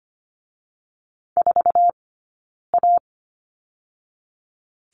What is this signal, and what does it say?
Morse code, heard as a keyed tone, about 25 wpm, spaced as for 9 wpm, 700 Hz. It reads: 4A
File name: Lesson 17d - Letter u - Two Characters 25wpm